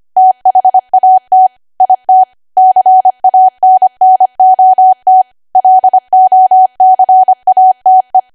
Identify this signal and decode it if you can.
Morse code, heard as a keyed tone, about 25 wpm, 750 Hz. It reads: THAT IT CANNOT LOCATE
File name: cwa / W1AW-Level3 / 25WPM